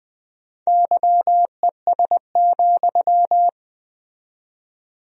Morse code, heard as a keyed tone, about 20 wpm, 700 Hz. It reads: YES,
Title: 4x Word-Spacing - Sentences from Top 500 Words - 20wpm